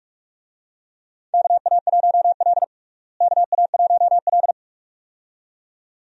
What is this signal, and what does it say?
Morse code, heard as a keyed tone, about 41 wpm, 700 Hz. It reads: KA1L KA1L
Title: USA Call Signs - 4 Characters - Repeated 2x - 45wpm